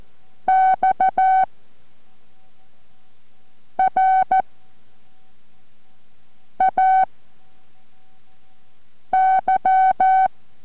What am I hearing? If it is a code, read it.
Morse code, heard as a keyed tone, about 14 wpm, 760 Hz. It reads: X R A Y